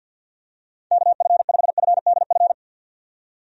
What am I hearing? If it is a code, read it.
Morse code, heard as a keyed tone, about 51 wpm, 700 Hz. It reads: KF5FDF
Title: USA Call Signs - 6 Characters - 50wpm